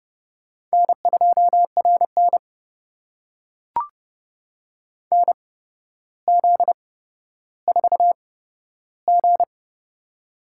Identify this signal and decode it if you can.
Morse code, heard as a keyed tone, about 30 wpm, spaced as for 13 wpm, 700 Hz. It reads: N2RN NZ4G
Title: USA Call Signs - 4 Characters - 8x Character Spacing - 30wpm